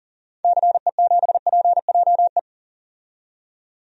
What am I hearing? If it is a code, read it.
Morse code, heard as a keyed tone, about 40 wpm, 700 Hz. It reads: CE7PJE
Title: DXCC-100 Call Signs - 6 Characters - 40wpm